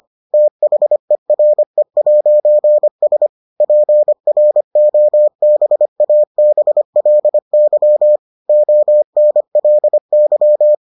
Morse code, heard as a keyed tone, about 25 wpm, 600 Hz. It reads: THERE'S PROBABLY ONLY